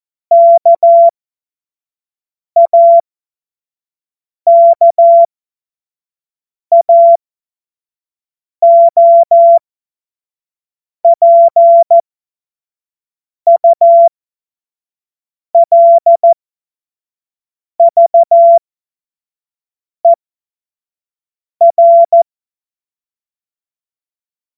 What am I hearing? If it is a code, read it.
Morse code, heard as a keyed tone, about 14 wpm, spaced as for 5 wpm, 680 Hz. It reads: KAKAOPULVER
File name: Morsekurs / Witze / Witz 07. September 2021, 25 F70